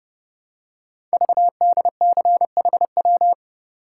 Morse code, heard as a keyed tone, about 29 wpm, 700 Hz. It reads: VDCHW
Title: Code Groups - 5 Letters - 30wpm